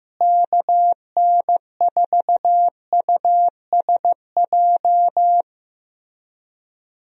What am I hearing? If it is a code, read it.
Morse code, heard as a keyed tone, about 15 wpm, 700 Hz. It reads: KN4USJ
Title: USA Call Signs - 6 Characters - Repeated 3x - 15wpm